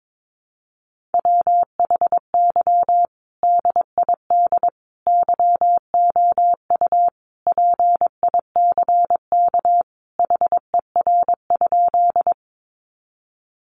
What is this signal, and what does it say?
Morse code, heard as a keyed tone, about 22 wpm, 700 Hz. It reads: WHY DID YOU PICK HER?
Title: Sentences from Top 1000 Words 22wpm